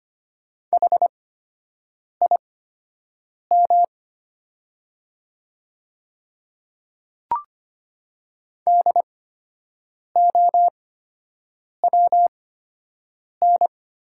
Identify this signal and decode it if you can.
Morse code, heard as a keyed tone, about 25 wpm, spaced as for 7 wpm, 700 Hz. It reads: HIM DOWN
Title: 8x Character Spacing - Top 100 Words - 25wpm